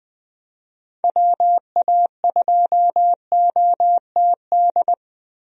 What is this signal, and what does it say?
Morse code, heard as a keyed tone, about 20 wpm, 700 Hz. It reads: WA2OTD
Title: USA Call Signs - 6 Characters - 20wpm